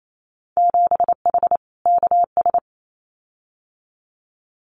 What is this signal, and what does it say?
Morse code, heard as a keyed tone, about 28 wpm, 700 Hz. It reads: 7H KS
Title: Winter Field Day Exchanges Encore 28wpm